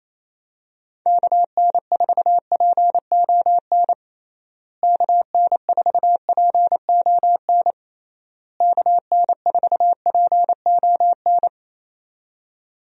Morse code, heard as a keyed tone, about 28 wpm, 700 Hz. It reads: KN4PON KN4PON KN4PON
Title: USA Call Signs - 6 Characters - Repeated 3x - 28wpm